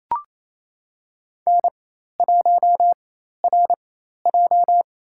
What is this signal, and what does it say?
Morse code, heard as a keyed tone, about 27 wpm, spaced as for 13 wpm, 700 Hz. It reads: N1RJ
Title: USA Call Signs - 4 Characters - 4x Character Spacing - 28wpm